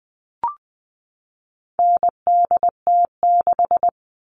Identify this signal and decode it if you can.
Morse code, heard as a keyed tone, about 20 wpm, 700 Hz. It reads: NDT6
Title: All Cut Numbers - 4-Digit 20wpm